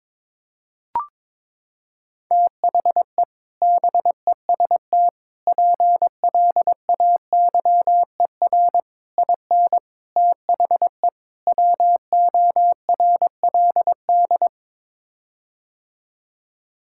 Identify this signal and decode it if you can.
Morse code, heard as a keyed tone, about 22 wpm, 700 Hz. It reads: THE BEST PLAYER IN THE WORLD
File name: practice / 22wpm / Sets of 6 Words - 22wpm